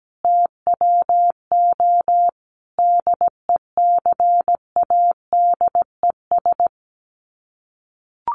Morse code, heard as a keyed tone, about 17 wpm, 700 Hz. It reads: TWO DECADES
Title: Sets of 5 Words Encore2 17wpm